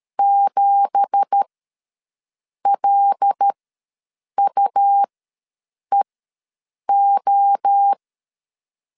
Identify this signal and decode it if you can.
Morse code, heard as a keyed tone, about 13 wpm, spaced as for 7 wpm, 800 Hz. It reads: 7LUEO
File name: LEZIONI / 10 SIGNS / 10D COMA ws 12wpm 3618